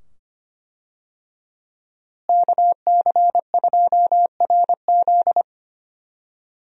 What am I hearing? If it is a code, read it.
Morse code, heard as a keyed tone, about 25 wpm, 700 Hz. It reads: KC2RZ
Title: USA Call Signs - 5 Characters - Repeated 2x - 25wpm